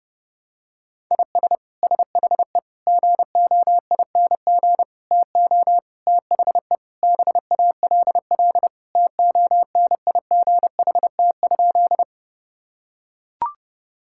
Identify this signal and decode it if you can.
Morse code, heard as a keyed tone, about 32 wpm, 700 Hz. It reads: IS SHE GOING TO THE BALL TONIGHT?
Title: Sentences from Top 1000 Words - Review Part 6 - 30wpm